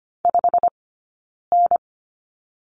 Morse code, heard as a keyed tone, about 25 wpm, spaced as for 9 wpm, 690 Hz. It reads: 5N
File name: Lesson 15d - Number 5 - Two Characters 25wpm